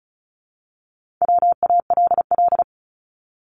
Morse code, heard as a keyed tone, about 36 wpm, 700 Hz. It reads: WALL
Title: Multi-letter Phonemes Repeated 3x 35wpm